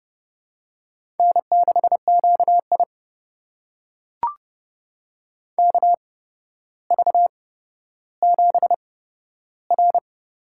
Morse code, heard as a keyed tone, about 30 wpm, spaced as for 13 wpm, 700 Hz. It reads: N6QI KV7R
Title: USA Call Signs - 4 Characters - 8x Character Spacing - 30wpm